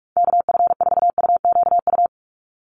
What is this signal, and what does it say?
Morse code, heard as a keyed tone, about 44 wpm, 700 Hz. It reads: KF4UQU
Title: USA Call Signs 45wpm